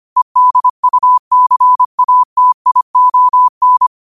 Morse code, heard as a keyed tone, about 25 wpm, 1000 Hz. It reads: EDUCATION
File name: qotdfiles / 25wpm